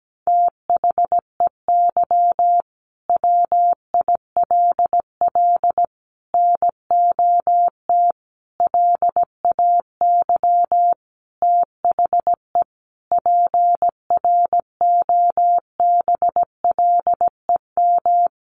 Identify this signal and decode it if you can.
Morse code, heard as a keyed tone, about 17 wpm, 700 Hz. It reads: THEY WILL NOT LAY THE PROBLEM
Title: Sentences from Top 1500 Words 17wpm